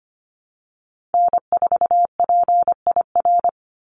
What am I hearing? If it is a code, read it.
Morse code, heard as a keyed tone, about 25 wpm, 700 Hz. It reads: N4PIR
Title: Call Signs Repeated 2x - 25wpm